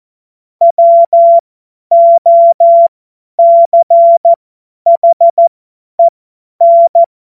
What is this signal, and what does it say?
Morse code, heard as a keyed tone, about 14 wpm, spaced as for 10 wpm, 680 Hz. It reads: WOCHEN
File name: Morsekurs / Witze / Witz 27. Juli 2021, 50 F70